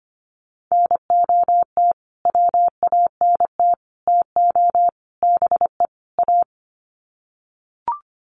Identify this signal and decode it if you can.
Morse code, heard as a keyed tone, about 25 wpm, 700 Hz. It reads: NOT WANT TO BE A
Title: Sets of 5 Words Encore 25wpm